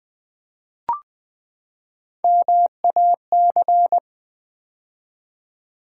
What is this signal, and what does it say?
Morse code, heard as a keyed tone, about 20 wpm, 700 Hz. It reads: MAC
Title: Build A Word - 3 Letters At A Time - 9-Letter Words - 20wpm